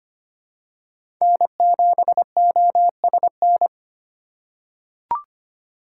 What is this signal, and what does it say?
Morse code, heard as a keyed tone, about 25 wpm, 700 Hz. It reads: N7OSN
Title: USA Call Signs - 4x Character Spacing - 25wpm